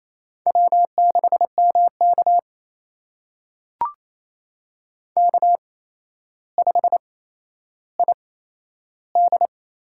Morse code, heard as a keyed tone, about 28 wpm, spaced as for 12 wpm, 700 Hz. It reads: W6MK K5ID
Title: USA Call Signs - 4 Characters - 8x Character Spacing - 28wpm